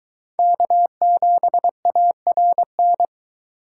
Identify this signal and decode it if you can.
Morse code, heard as a keyed tone, about 24 wpm, 700 Hz. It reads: K7ARN